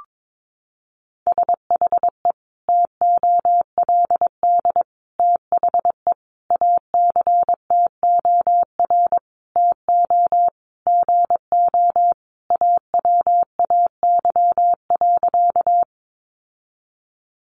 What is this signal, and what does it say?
Morse code, heard as a keyed tone, about 22 wpm, 700 Hz. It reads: SHE TOLD THE ACTOR TO GO AWAY.
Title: Sentences from Top 1500 Words 22wpm